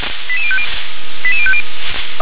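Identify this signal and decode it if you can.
Five-tone selective call (ZVEI), heard as quick runs of five tones, tones of about 70 ms. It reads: 88050 88050
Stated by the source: ZVEI five-tone selective call 88101 received by SDR.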